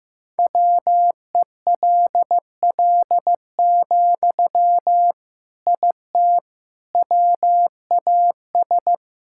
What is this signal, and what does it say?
Morse code, heard as a keyed tone, about 15 wpm, 700 Hz. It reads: WELL, IT WAS
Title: Sentences from Top 1000 Words - Review Part 6 - 15wpm